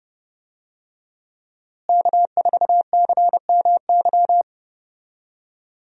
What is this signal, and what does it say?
Morse code, heard as a keyed tone, about 29 wpm, 700 Hz. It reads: K4CMY
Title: USA Call Signs - 5 Characters - Repeated 2x - 30wpm